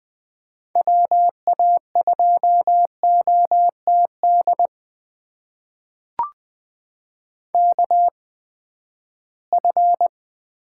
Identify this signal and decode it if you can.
Morse code, heard as a keyed tone, about 20 wpm, spaced as for 14 wpm, 700 Hz. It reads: WA2OTD KF
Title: USA Call Signs - 6 Characters - 8x Character Spacing - 20wpm